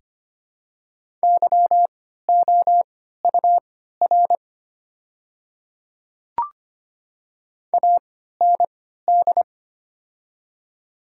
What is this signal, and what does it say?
Morse code, heard as a keyed tone, about 25 wpm, spaced as for 14 wpm, 700 Hz. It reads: YOUR AND
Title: 3x Character Spacing - Top 100 Words - 25wpm